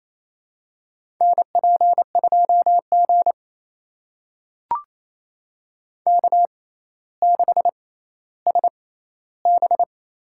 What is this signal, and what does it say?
Morse code, heard as a keyed tone, about 28 wpm, spaced as for 14 wpm, 700 Hz. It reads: NP2G K6SB
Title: USA Call Signs - 4 Characters - 6x Character Spacing - 28wpm